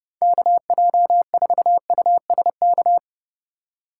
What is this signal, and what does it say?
Morse code, heard as a keyed tone, about 30 wpm, 700 Hz. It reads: KJ4USK
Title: USA Call Signs - 6 Characters - 2x Character Spacing - 30wpm